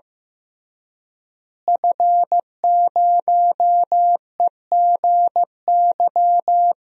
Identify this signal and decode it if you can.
Morse code, heard as a keyed tone, about 15 wpm, 700 Hz. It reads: F0EGY
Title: DXCC-100 Call Signs - 5 Characters - Repeated 2x - 15wpm